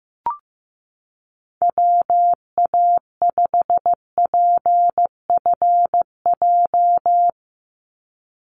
Morse code, heard as a keyed tone, about 15 wpm, 700 Hz. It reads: WA5PFJ
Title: Call Signs Repeated 3x - 15wpm